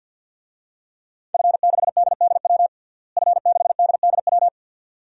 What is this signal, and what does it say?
Morse code, heard as a keyed tone, about 51 wpm, 700 Hz. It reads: F6DDW F6DDW
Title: DXCC-100 Call Signs - 5 Characters - Repeated 2x - 50wpm